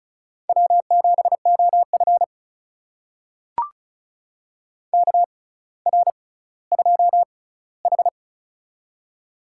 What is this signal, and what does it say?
Morse code, heard as a keyed tone, about 36 wpm, spaced as for 18 wpm, 700 Hz. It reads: W7OF KR2H
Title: USA Call Signs - 4 Characters - 6x Character Spacing - 35wpm